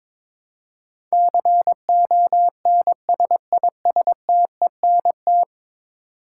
Morse code, heard as a keyed tone, about 22 wpm, 700 Hz. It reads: CONSISTENT
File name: practice / 22wpm / 10 Letter Words - First 3 Letters Given - 22wpm